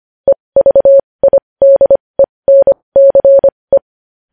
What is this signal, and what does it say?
Morse code, heard as a keyed tone, about 25 wpm, spaced as for 20 wpm, 550 Hz. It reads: EVIDENCE
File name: podcast / News headlines-1x-25x20